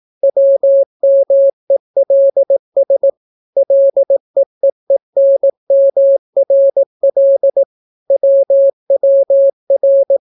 Morse code, heard as a keyed tone, about 18 wpm, 550 Hz. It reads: WMELS LEEENMRL WWR